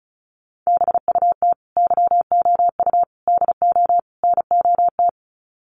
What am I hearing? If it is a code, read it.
Morse code, heard as a keyed tone, about 34 wpm, 700 Hz. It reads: BUT YOU DO NOT
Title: Sets of 4 Words 35wpm